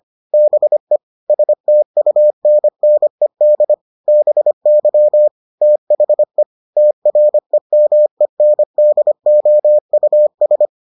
Morse code, heard as a keyed tone, about 25 wpm, 600 Hz. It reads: BE STUNNED BY THE TREMENDOUS